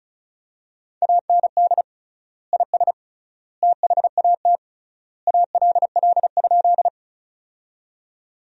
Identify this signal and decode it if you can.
Morse code, heard as a keyed tone, about 34 wpm, 700 Hz. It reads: AND IS THAT ALL?
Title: 3x Word-Spacing - Sentences from Top 100 Words - 35wpm